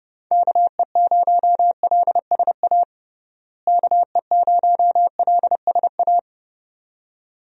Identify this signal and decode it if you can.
Morse code, heard as a keyed tone, about 32 wpm, 700 Hz. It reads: KE0LSA KE0LSA
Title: USA Call Signs - 6 Characters - Repeated 2x - 30wpm